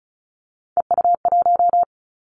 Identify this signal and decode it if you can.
Morse code, heard as a keyed tone, about 34 wpm, 700 Hz. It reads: EU1
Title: Callsign Prefixes for Top 50 DXCC - 35wpm